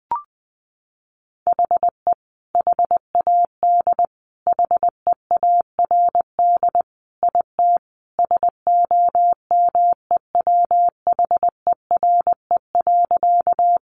Morse code, heard as a keyed tone, about 20 wpm, 700 Hz. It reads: HE HAD HEARD IT SOMEWHERE.
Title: Sentences from Top 1400 Words 20wpm